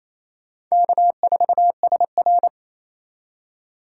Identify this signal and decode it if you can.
Morse code, heard as a keyed tone, about 37 wpm, 700 Hz. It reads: K4SR